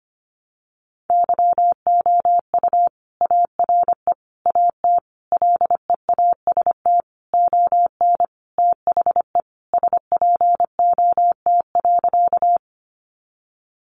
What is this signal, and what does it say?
Morse code, heard as a keyed tone, about 25 wpm, 700 Hz. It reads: YOU ARE AT LEAST ON THE SPOT.
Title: Sentences from Top 1100 Words 25wpm